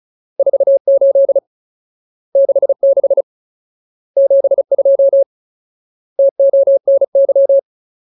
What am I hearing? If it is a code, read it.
Morse code, heard as a keyed tone, about 34 wpm, 550 Hz. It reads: 48 66 72 TONY
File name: Ultimate Word List / 35WPM Ultimate